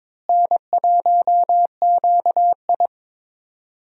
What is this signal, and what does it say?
Morse code, heard as a keyed tone, about 22 wpm, 700 Hz. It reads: N1QI
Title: USA Call Signs - 4 Characters - Repeated 3x - 22wpm